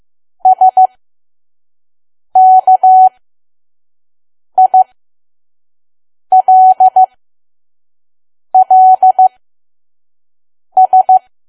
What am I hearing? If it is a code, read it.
Morse code, heard as a keyed tone, about 15 wpm, spaced as for 5 wpm, 750 Hz. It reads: SKILLS
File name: Archive / 5 WPM files / PM